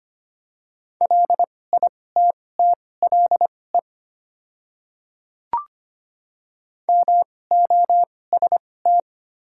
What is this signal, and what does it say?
Morse code, heard as a keyed tone, about 25 wpm, spaced as for 18 wpm, 700 Hz. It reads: LITTLE MOST
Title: 2x Character Spacing - Top 300 Words - 25wpm